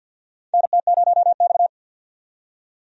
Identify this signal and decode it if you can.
Morse code, heard as a keyed tone, about 48 wpm, 700 Hz. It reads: NT0X
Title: USA Call Signs - 4 Characters - 50wpm